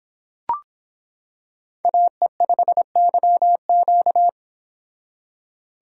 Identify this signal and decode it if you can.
Morse code, heard as a keyed tone, about 25 wpm, 700 Hz. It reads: AE5YQ